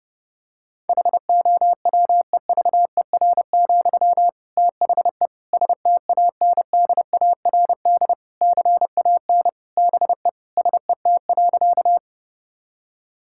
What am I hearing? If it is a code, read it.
Morse code, heard as a keyed tone, about 30 wpm, 700 Hz. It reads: HOWEVER, THE STANDARD CAN BE SET.
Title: Sentences from Top 1000 Words - Review Part 6 - 30wpm